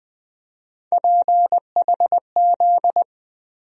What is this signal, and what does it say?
Morse code, heard as a keyed tone, about 20 wpm, 700 Hz. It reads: PHZ
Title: USA Call Sign Postfixes - 3 Letters 20wpm